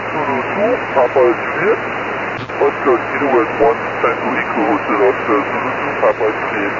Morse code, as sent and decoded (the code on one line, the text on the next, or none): none